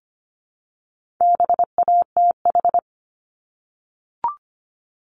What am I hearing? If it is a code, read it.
Morse code, heard as a keyed tone, about 25 wpm, 700 Hz. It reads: BATH
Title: Multi-letter Phonemes Repeated 3x 25wpm